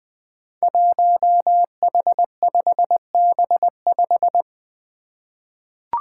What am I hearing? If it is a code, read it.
Morse code, heard as a keyed tone, about 21 wpm, 700 Hz. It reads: 1H5B5
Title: Five Challenging Letter-Number Combinations 20wpm